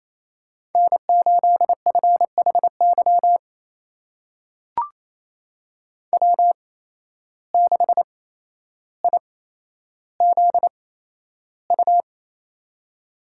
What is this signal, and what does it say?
Morse code, heard as a keyed tone, about 28 wpm, spaced as for 12 wpm, 700 Hz. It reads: N8FHY W6IZU
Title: USA Call Signs - 5 Characters - 8x Character Spacing - 28wpm